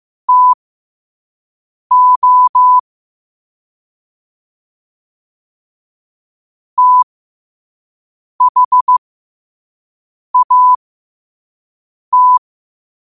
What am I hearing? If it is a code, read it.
Morse code, heard as a keyed tone, about 15 wpm, spaced as for 5 wpm, 1000 Hz. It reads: TO THAT
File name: qotdfiles / 5wpm